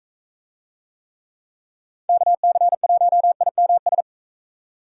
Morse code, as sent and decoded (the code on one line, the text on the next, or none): -.- -.-. .---- .. -- ...
KC1IMS